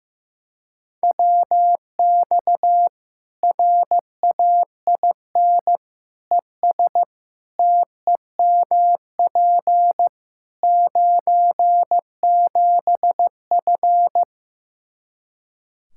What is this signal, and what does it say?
Morse code, heard as a keyed tone, about 15 wpm, 700 Hz. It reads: WX RAIN ES TEMP 97F
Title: Practice QSO - Exchange 2 - Info - 15wpm